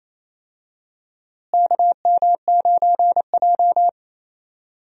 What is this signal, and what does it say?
Morse code, heard as a keyed tone, about 28 wpm, 700 Hz. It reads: KM9J
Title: USA Call Signs - 4 Characters - Repeated 3x - 28wpm